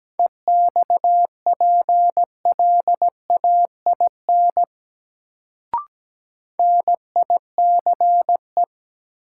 Morse code, heard as a keyed tone, about 17 wpm, 700 Hz. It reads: EXPLAIN NICE
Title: Sentences from Top 1100 Words - New Word Drill Alt - 17wpm